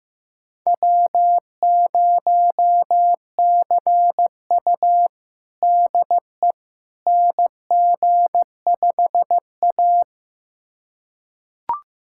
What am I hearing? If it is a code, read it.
Morse code, heard as a keyed tone, about 15 wpm, 700 Hz. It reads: W0CU DE NG5A
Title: USA Callsign DE Callsign - 4 Characters - 15wpm